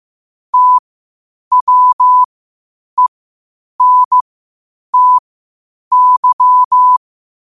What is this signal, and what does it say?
Morse code, heard as a keyed tone, about 15 wpm, spaced as for 8 wpm, 1000 Hz. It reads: TWENTY
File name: qotdfiles / 8wpm